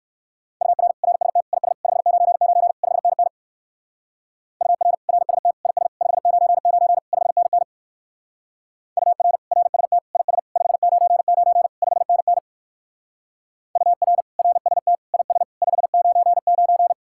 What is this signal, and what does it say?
Morse code, heard as a keyed tone, about 58 wpm, 700 Hz. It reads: UR RST IS 599 5NN UR RST IS 599 5NN UR RST IS 599 5NN UR RST IS 599